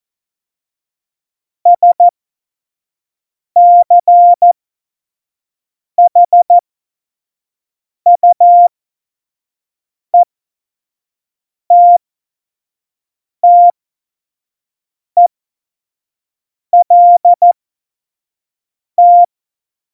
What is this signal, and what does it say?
Morse code, heard as a keyed tone, about 14 wpm, spaced as for 5 wpm, 700 Hz. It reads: SCHUETTELT
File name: Morsekurs / Witze / Witz 15. Mai 2018, 25 F70